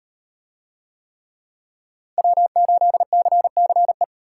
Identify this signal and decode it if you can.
Morse code, heard as a keyed tone, about 37 wpm, 700 Hz. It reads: W8CCE